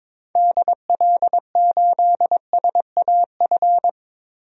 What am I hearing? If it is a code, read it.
Morse code, heard as a keyed tone, about 21 wpm, 700 Hz. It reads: DL8SAF
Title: DXCC-100 Call Signs - 6 Characters - Repeated 3x - 22wpm